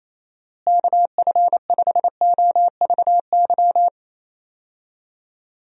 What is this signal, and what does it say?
Morse code, heard as a keyed tone, about 28 wpm, 700 Hz. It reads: KF5OVY
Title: USA Call Signs - 6 Characters - Repeated 2x - 28wpm